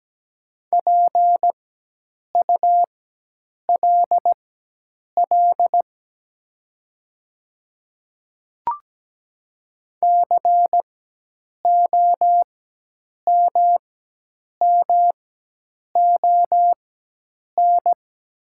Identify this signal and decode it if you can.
Morse code, heard as a keyed tone, about 17 wpm, spaced as for 8 wpm, 700 Hz. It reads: PULL COMMON
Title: 4x Character Spacing - Top 500 Words - 17wpm